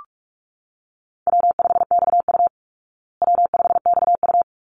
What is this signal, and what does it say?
Morse code, heard as a keyed tone, about 44 wpm, 700 Hz. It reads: W5XU W5XU
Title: Easy Call Signs Repeated 2x - 45wpm